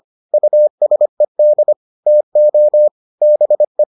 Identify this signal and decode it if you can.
Morse code, heard as a keyed tone, about 25 wpm, 600 Hz. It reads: USED TO BE